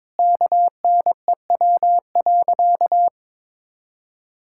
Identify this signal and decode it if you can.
Morse code, heard as a keyed tone, about 22 wpm, 700 Hz. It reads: KNEW.
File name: practice / 22wpm / Sentences with 3-Word Phrases - 22wpm